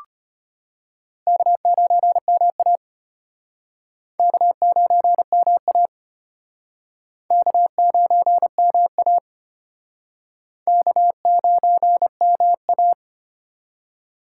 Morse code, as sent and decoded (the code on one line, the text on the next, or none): -.- ----. -- .- / -.- ----. -- .- / -.- ----. -- .- / -.- ----. -- .-
K9MA K9MA K9MA K9MA